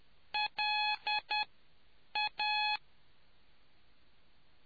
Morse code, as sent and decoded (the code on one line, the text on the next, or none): .-.. .-
LA